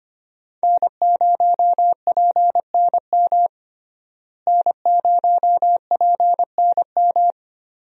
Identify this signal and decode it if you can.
Morse code, heard as a keyed tone, about 25 wpm, 700 Hz. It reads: N0PNM N0PNM
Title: USA Call Signs - 5 Characters - Repeated 2x - 25wpm